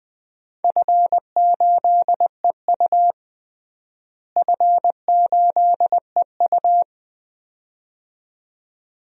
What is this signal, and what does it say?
Morse code, heard as a keyed tone, about 20 wpm, 700 Hz. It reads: F8EU F8EU
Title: DXCC-100 Call Signs - 4 Characters - Repeated 2x - 20wpm